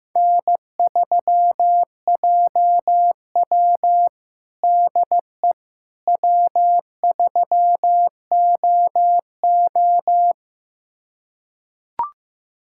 Morse code, heard as a keyed tone, about 15 wpm, 700 Hz. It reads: N3JW DE W3OO
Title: USA Callsign DE Callsign - 4 Characters - 15wpm